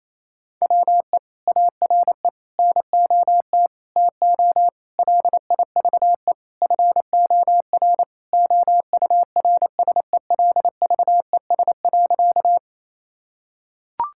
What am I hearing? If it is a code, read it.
Morse code, heard as a keyed tone, about 28 wpm, 700 Hz. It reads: WE ARE NOT TO LIVE FOR OURSELVES.
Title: Sentences from Top 1600 Words 28wpm